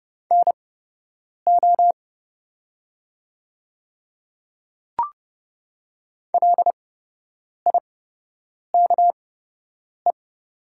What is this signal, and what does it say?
Morse code, heard as a keyed tone, about 31 wpm, spaced as for 8 wpm, 690 Hz. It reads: NO LIKE
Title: 8x Character Spacing - Top 100 Words - 30wpm